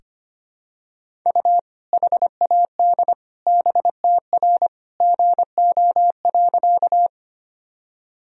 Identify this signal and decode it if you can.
Morse code, heard as a keyed tone, about 25 wpm, 700 Hz. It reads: U HAD BTR GO.
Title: Sentences from Top 400 Words and Abbreviations 25wpm